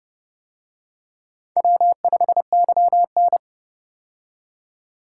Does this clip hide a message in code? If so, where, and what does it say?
Yes: Morse code from 1.6 to 3.4 s, heard as a keyed tone, about 30 wpm, 700 Hz. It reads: W5YN